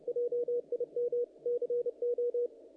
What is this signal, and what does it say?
Morse code, heard as a keyed tone, about 30 wpm, 480 Hz, with radio noise. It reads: JIM CO
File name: contests / naqp-cw-k5tr-op-k5pi- -1900